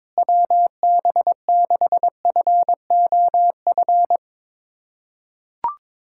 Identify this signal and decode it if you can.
Morse code, heard as a keyed tone, about 22 wpm, 700 Hz. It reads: WB6FOF
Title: USA Callsign DE Callsign 22wpm